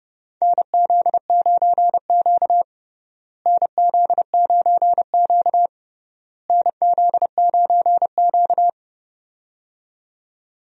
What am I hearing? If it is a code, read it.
Morse code, heard as a keyed tone, about 31 wpm, 700 Hz. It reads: NZ9Q NZ9Q NZ9Q
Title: USA Call Signs - 4 Characters - Repeated 3x - 30wpm